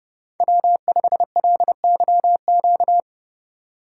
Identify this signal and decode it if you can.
Morse code, heard as a keyed tone, about 30 wpm, 700 Hz. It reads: W5LYQ